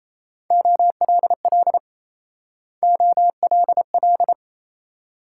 Morse code, heard as a keyed tone, about 36 wpm, 700 Hz. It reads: OLL OLL